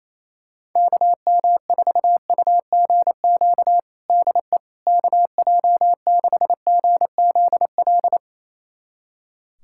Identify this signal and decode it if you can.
Morse code, heard as a keyed tone, about 28 wpm, 700 Hz. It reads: KM4UGQ DE KJ6GZL
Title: USA Callsign DE Callsign 28wpm